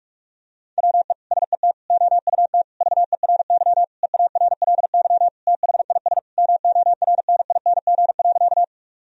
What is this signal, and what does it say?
Morse code, heard as a keyed tone, about 46 wpm, 700 Hz. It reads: WE SET OUT VERY EARLY THIS MORNING.
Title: Sentences from Top 1000 Words - Review Part 2 - 45wpm